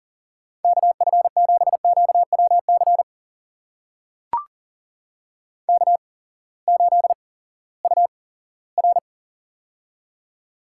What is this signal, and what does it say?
Morse code, heard as a keyed tone, about 40 wpm, spaced as for 20 wpm, 700 Hz. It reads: KF7QWC K8UR
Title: USA Call Signs - 8x Character Spacing - 40wpm